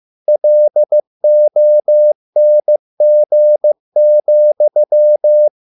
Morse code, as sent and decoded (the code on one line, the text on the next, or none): .-.. --- -. --. --..--
LONG,